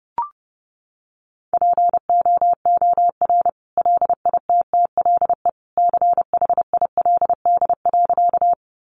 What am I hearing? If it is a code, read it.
Morse code, heard as a keyed tone, about 30 wpm, 700 Hz. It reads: POOR LITTLE CHILD.
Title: Sentences from Top 1000 Words 30wpm